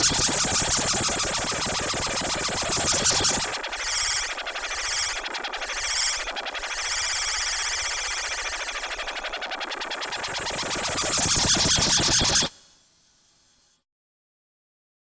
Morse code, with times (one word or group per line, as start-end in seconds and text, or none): none